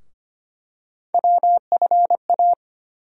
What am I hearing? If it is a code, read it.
Morse code, heard as a keyed tone, about 25 wpm, 700 Hz. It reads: WFA
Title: USA Call Sign Postfixes - 3 Letters 25wpm